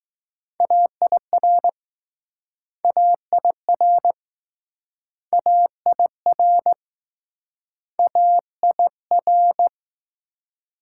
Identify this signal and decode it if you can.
Morse code, heard as a keyed tone, about 20 wpm, 700 Hz. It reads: AIR AIR AIR AIR